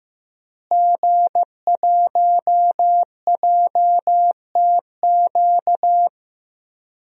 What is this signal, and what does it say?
Morse code, heard as a keyed tone, about 15 wpm, 700 Hz. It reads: G1JTQ
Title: DXCC-100 Call Signs - 5 Characters - Repeated 3x - 15wpm